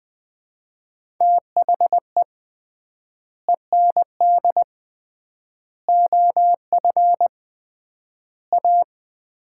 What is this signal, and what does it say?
Morse code, heard as a keyed tone, about 21 wpm, 700 Hz. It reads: THE END OF A
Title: 3x Word-Spacing - Sets of 4 Words - 20wpm